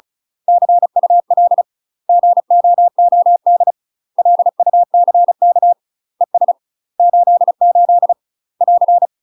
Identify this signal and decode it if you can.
Morse code, heard as a keyed tone, about 36 wpm, 700 Hz. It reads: CUL GOOD LUCK ES 88 <AR>
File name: UK MASTER QSOs TRAINING / 35WPM / UK MASTER QSOs 0038